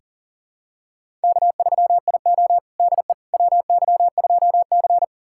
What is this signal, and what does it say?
Morse code, heard as a keyed tone, about 37 wpm, 700 Hz. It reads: K3IO DE WY2C
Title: USA Callsign DE Callsign - 4 Characters - 40wpm